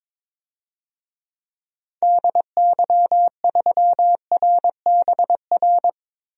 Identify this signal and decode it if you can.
Morse code, heard as a keyed tone, about 22 wpm, 700 Hz. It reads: DY3RBR
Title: DXCC-100 Call Signs - 6 Characters - Repeated 2x - 22wpm